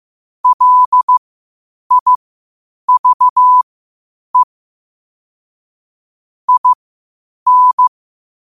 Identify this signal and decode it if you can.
Morse code, heard as a keyed tone, about 15 wpm, spaced as for 8 wpm, 1000 Hz. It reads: LIVE IN
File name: qotdfiles / 8wpm